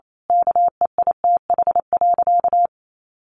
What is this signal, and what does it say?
Morse code, heard as a keyed tone, about 28 wpm, 700 Hz. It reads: KEITH.
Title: A Sound Of Thunder 28wpm